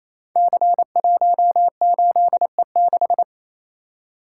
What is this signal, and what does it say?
Morse code, heard as a keyed tone, about 29 wpm, 700 Hz. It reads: C18E6
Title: Code Groups - 5 Letter:Numbers - 28wpm